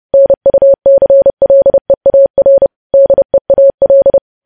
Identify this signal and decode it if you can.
Morse code, heard as a keyed tone, about 30 wpm, 550 Hz. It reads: NUCLEAR DEAL
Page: news headlines in morse code at 30 wpm